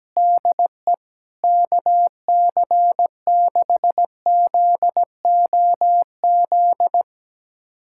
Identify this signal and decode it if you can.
Morse code, heard as a keyed tone, about 17 wpm, 700 Hz. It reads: DE KC6ZOZ
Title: USA Callsign DE Callsign - 6 Characters - 17wpm